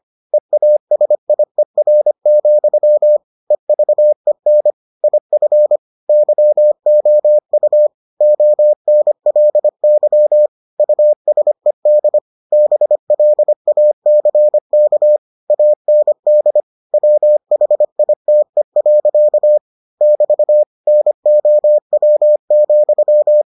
Morse code, heard as a keyed tone, about 25 wpm, 600 Hz. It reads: EASIER, EVEN IF YOU ONLY USED BLACK AND WHITE. <BT> NOW,